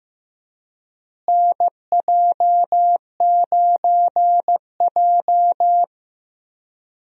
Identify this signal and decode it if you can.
Morse code, heard as a keyed tone, about 15 wpm, 700 Hz. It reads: NJ9J
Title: USA Call Signs - 4 Characters - Repeated 2x - 15wpm